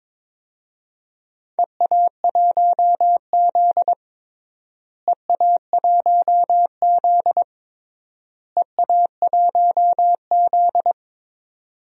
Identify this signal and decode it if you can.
Morse code, heard as a keyed tone, about 21 wpm, 700 Hz. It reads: EA1Z EA1Z EA1Z
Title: DXCC-100 Call Signs - 4 Characters - Repeated 3x - 22wpm